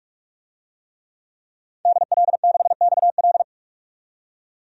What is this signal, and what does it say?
Morse code, heard as a keyed tone, about 45 wpm, 700 Hz. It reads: DL6XL
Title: DXCC-100 Call Signs - 5 Characters - 45wpm